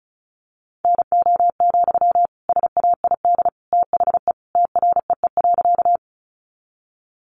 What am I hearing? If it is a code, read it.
Morse code, heard as a keyed tone, about 34 wpm, 700 Hz. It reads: NO, SAID THE TREE.
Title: Sentences from Top 300 Words 35wpm